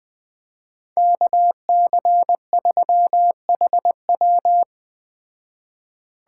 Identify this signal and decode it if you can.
Morse code, heard as a keyed tone, about 19 wpm, 700 Hz. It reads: KC3HW